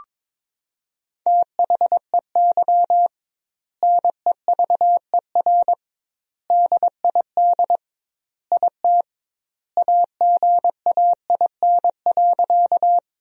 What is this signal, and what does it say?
Morse code, heard as a keyed tone, about 22 wpm, 700 Hz. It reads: THEY NEVER DID IT AGAIN.
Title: 2x Word-Spacing - Sentences from Top 300 Words - 22wpm